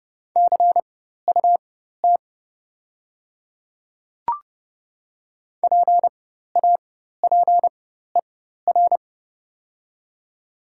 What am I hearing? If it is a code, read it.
Morse code, heard as a keyed tone, about 31 wpm, spaced as for 14 wpm, 700 Hz. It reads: CUT PAPER
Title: 4x Character Spacing - Top 300 Words - 30wpm